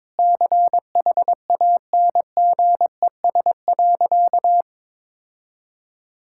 Morse code, heard as a keyed tone, about 22 wpm, 700 Hz. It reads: CHANGES.
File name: practice / 22wpm / Sentences with 3-Word Phrases - 22wpm